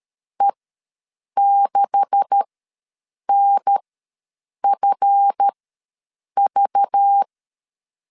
Morse code, heard as a keyed tone, about 13 wpm, spaced as for 7 wpm, 800 Hz. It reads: E6NFV
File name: LEZIONI / 9 286 ws 12wpm 3351